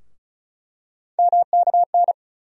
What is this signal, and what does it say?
Morse code, heard as a keyed tone, about 36 wpm, 700 Hz. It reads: MKN
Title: USA Call Sign Postfixes - 3 Letters 35wpm